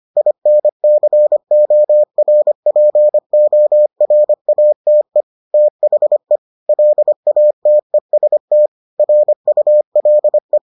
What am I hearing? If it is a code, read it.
Morse code, heard as a keyed tone, about 25 wpm, 600 Hz. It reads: INCORPORATE THE LATEST RULE